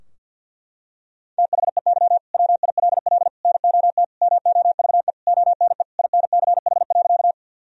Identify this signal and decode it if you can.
Morse code, heard as a keyed tone, about 50 wpm, 700 Hz. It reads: THEY WILL NOT MOVE ONE INCH.
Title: Sentences from Top 1000 Words - Review Part 3 - 50wpm